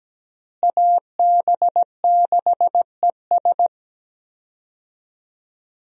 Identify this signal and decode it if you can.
Morse code, heard as a keyed tone, about 17 wpm, 700 Hz. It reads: AB6ES
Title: USA Call Signs - 5 Characters - Repeated 2x - 17wpm